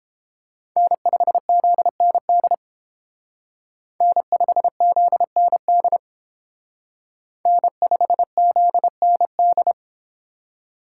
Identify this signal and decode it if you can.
Morse code, heard as a keyed tone, about 31 wpm, 700 Hz. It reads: N5ZND N5ZND N5ZND